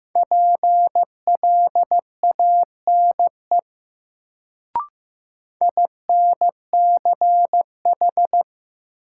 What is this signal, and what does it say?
Morse code, heard as a keyed tone, about 15 wpm, 700 Hz. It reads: PLANE INCH
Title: Sentences from Top 1000 Words - Review Part 3 - New Word Drill - 15wpm